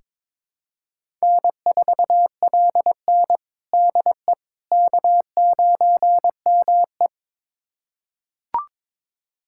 Morse code, heard as a keyed tone, about 22 wpm, 700 Hz. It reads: N4LN DE K9ME
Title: USA Callsign DE Callsign - 4 Characters - 22wpm